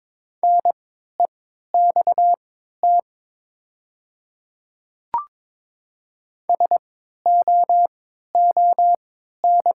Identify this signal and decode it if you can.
Morse code, heard as a keyed tone, about 22 wpm, spaced as for 13 wpm, 700 Hz. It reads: NEXT SOON
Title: 3x Character Spacing - Top 500 Words - 22wpm